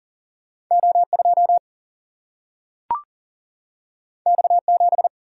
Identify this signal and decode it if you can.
Morse code, heard as a keyed tone, about 38 wpm, 700 Hz. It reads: O 2 X 7
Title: Single-Letter Single-Number 40wpm